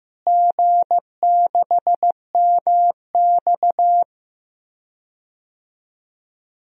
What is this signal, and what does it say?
Morse code, heard as a keyed tone, about 15 wpm, 700 Hz. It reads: G6MX
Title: DXCC-100 Call Signs - 4 Characters - Repeated 3x - 15wpm